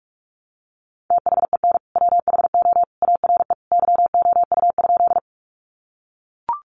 Morse code, heard as a keyed tone, about 44 wpm, 700 Hz. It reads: THEN WHO ARE YOU?
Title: Sentences from Top 100 Words 45wpm